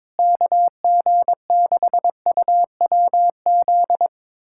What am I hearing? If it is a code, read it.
Morse code, heard as a keyed tone, about 22 wpm, 700 Hz. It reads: KG6UWZ
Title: USA Callsign DE Callsign 22wpm